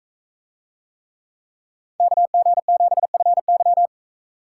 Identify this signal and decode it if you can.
Morse code, heard as a keyed tone, about 41 wpm, 700 Hz. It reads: KG7FY